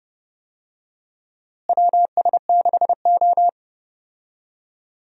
Morse code, heard as a keyed tone, about 30 wpm, 700 Hz. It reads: WS6O
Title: USA Call Signs - 4 Characters - Repeated 3x - 30wpm